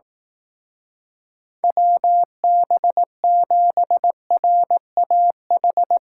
Morse code, heard as a keyed tone, about 18 wpm, 700 Hz. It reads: WB7RAH
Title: Speed Racing - USA 2x3 Call Signs 15wpm